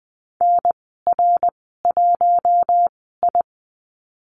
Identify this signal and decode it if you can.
Morse code, heard as a keyed tone, about 20 wpm, spaced as for 15 wpm, 700 Hz. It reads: NR1I
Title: 147 Lesson 24g - Letter y - Call Signs 20wpm